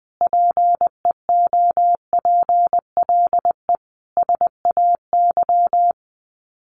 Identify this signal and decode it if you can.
Morse code, heard as a keyed tone, about 20 wpm, 700 Hz. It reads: PEOPLE SAY
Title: Sets of 5 Words Encore2 20wpm